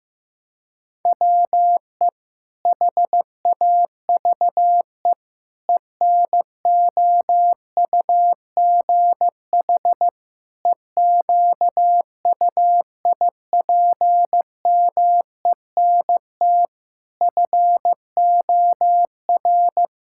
Morse code, heard as a keyed tone, about 14 wpm, 700 Hz. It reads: WE HAVE ENOUGH EQUIPMENT FOR
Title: Sentences from Top 1000 Words - Review Part 5 - 15wpm